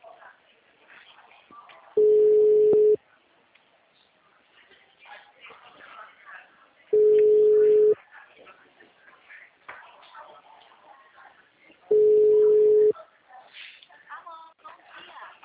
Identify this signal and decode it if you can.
Ringback tone (Europe, 425 Hz), with speech.
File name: out- -2019- - - .4